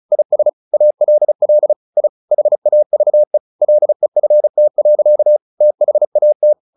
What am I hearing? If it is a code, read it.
Morse code, heard as a keyed tone, about 34 wpm, 600 Hz. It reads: IS ALL I HAVE LEFT. THAT
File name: the old man and the sea 2 35wpm